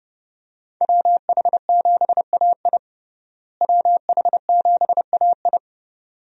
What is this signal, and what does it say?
Morse code, heard as a keyed tone, about 32 wpm, 700 Hz. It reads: WH7AI WH7AI
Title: USA Call Signs - 5 Characters - Repeated 2x - 30wpm